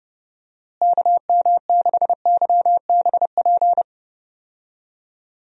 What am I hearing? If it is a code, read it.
Morse code, heard as a keyed tone, about 30 wpm, 700 Hz. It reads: KM6YBP
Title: USA Call Signs - 6 Characters - Repeated 3x - 30wpm